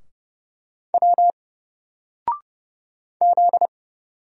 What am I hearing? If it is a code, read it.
Morse code, heard as a keyed tone, about 31 wpm, 700 Hz. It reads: W Z
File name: Single-Letter-Number-30wpm